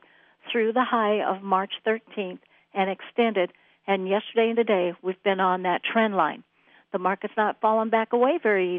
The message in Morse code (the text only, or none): none